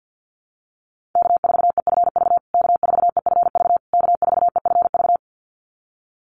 Morse code, heard as a keyed tone, about 50 wpm, 700 Hz. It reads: K4EFV K4EFV K4EFV
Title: USA Call Signs Repeated 3x 50wpm